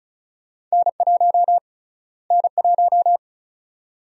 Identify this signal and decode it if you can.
Morse code, heard as a keyed tone, about 34 wpm, 700 Hz. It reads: N1 N1
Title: USA Call Sign Prefixes - Repeated 2x - 35wpm